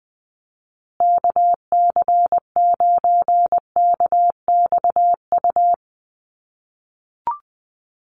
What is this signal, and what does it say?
Morse code, heard as a keyed tone, about 20 wpm, 700 Hz. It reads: KC9KXU
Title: USA Call Signs Repeated 3x 20wpm